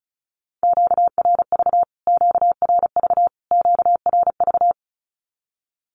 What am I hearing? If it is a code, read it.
Morse code, heard as a keyed tone, about 34 wpm, 700 Hz. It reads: QRV QRV QRV
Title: Q-Codes Repeated 3x 35wpm